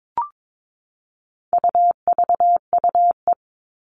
Morse code, heard as a keyed tone, about 22 wpm, 700 Hz. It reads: UVUE
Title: All Cut Numbers - 4-Digit 22wpm